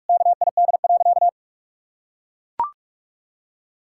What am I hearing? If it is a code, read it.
Morse code, heard as a keyed tone, about 42 wpm, 700 Hz. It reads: KID.
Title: Sentences from Top 1600 Words 45wpm